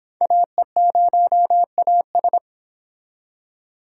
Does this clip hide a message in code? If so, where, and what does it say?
Yes: Morse code from 0.2 to 2.4 s, heard as a keyed tone, about 26 wpm, 700 Hz. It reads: AE0AS